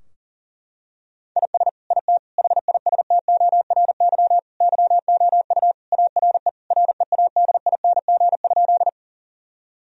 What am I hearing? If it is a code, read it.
Morse code, heard as a keyed tone, about 40 wpm, 700 Hz. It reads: IS IT HISTORY YOU ARE READING?
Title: Sentences from Top 1000 Words - Review Part 3 - 40wpm